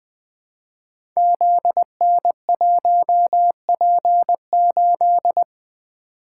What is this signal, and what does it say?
Morse code, heard as a keyed tone, about 20 wpm, 700 Hz. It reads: ZN1P8
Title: Code Groups - 5 Letter:Numbers - 20wpm